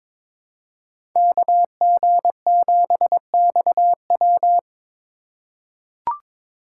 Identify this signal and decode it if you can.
Morse code, heard as a keyed tone, about 22 wpm, 700 Hz. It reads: KG7XW
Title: USA Call Signs - 5 Characters - 6x Character Spacing - 22wpm